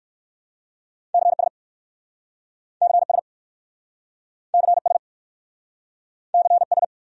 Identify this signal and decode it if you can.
Morse code, heard as a keyed tone, about 60 wpm, 700 Hz. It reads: CS CS CS CS